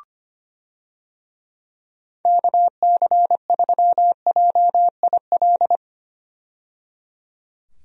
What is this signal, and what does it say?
Morse code, heard as a keyed tone, about 25 wpm, 700 Hz. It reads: KC3JIL
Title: USA Call Signs - 6 Characters - 25wpm